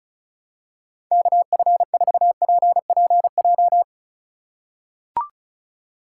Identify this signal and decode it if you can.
Morse code, heard as a keyed tone, about 36 wpm, 700 Hz. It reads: KF4PPJ
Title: USA Call Signs - 3x Character Spacing - 35wpm